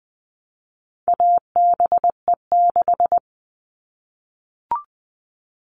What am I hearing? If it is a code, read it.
Morse code, heard as a keyed tone, about 20 wpm, 700 Hz. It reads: ABE6
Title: All Cut Numbers - 4-Digit 20wpm